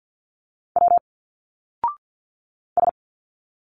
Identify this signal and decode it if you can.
Morse code, heard as a keyed tone, about 50 wpm, 700 Hz. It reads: W S
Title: Single Letters 50wpm